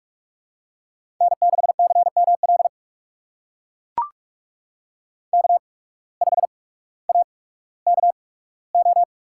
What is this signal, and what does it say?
Morse code, heard as a keyed tone, about 44 wpm, spaced as for 19 wpm, 700 Hz. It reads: N6CML K5AKO
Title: USA Call Signs - 8x Character Spacing - 45wpm